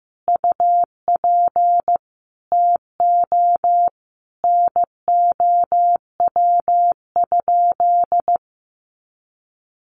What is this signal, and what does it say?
Morse code, heard as a keyed tone, about 15 wpm, 700 Hz. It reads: UP TO NOW?
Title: Sentences from Top 100 Words 15wpm